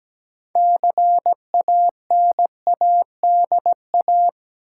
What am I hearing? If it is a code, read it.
Morse code, heard as a keyed tone, about 17 wpm, 700 Hz. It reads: CANADA
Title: Sets of 5 Words - New Word Drill - 17wpm